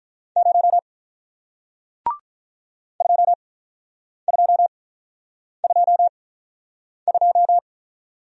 Morse code, heard as a keyed tone, about 48 wpm, spaced as for 9 wpm, 700 Hz. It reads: 0 2222